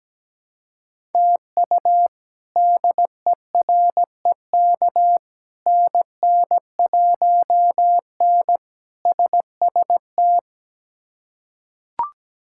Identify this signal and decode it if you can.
Morse code, heard as a keyed tone, about 17 wpm, 700 Hz. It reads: TU DEREK NN1N SST
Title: K1USN SST QSO Practice - 17wpm